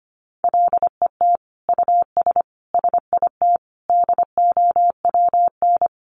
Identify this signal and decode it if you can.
Morse code, heard as a keyed tone, about 25 wpm, 700 Hz. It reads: LET US SIT DOWN
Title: Sentences from Top 1400 Words 25wpm